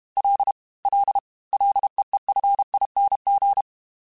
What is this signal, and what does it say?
Morse code, heard as a keyed tone, about 32 wpm, 800 Hz. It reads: L L LEEFING
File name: Ls / Single L m3s / Single L @30